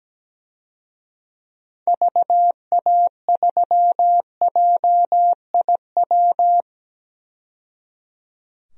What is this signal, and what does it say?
Morse code, heard as a keyed tone, about 17 wpm, 700 Hz. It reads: VA3JIW
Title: DXCC-100 Call Signs - 6 Characters - 17wpm